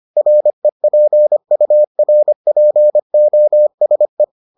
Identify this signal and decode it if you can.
Morse code, heard as a keyed tone, about 25 wpm, 600 Hz. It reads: REPURPOSE